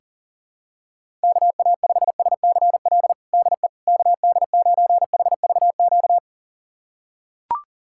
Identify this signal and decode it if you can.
Morse code, heard as a keyed tone, about 40 wpm, 690 Hz. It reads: KA5SCL DE KD9HVQ
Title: USA Callsign DE Callsign - 6 Characters - 40wpm